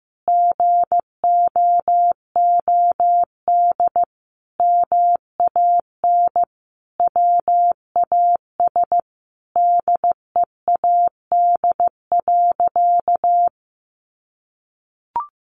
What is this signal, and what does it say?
Morse code, heard as a keyed tone, about 15 wpm, 700 Hz. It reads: GOOD MAN WAS DEAD.
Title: Sentences from Top 1500 Words 15wpm